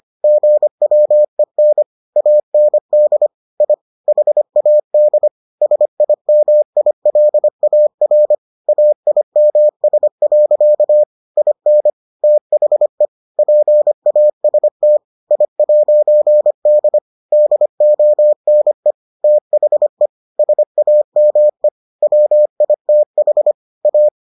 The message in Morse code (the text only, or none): GWEN AND I HAD SIMILAR AIMS. IN THE PAST I'D DONE THE SAME WITH A